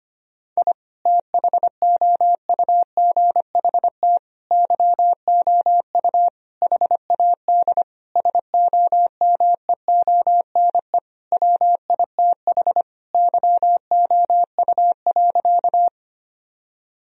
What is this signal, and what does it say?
Morse code, heard as a keyed tone, about 25 wpm, 700 Hz. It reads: I THOUGHT YOU HAD SOMEONE WITH YOU.
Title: Sentences from Top 1000 Words - Review Part 4 - 25wpm